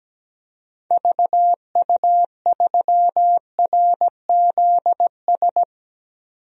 Morse code, heard as a keyed tone, about 17 wpm, 700 Hz. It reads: VU3RZS
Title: DXCC-100 Call Signs - 6 Characters - 17wpm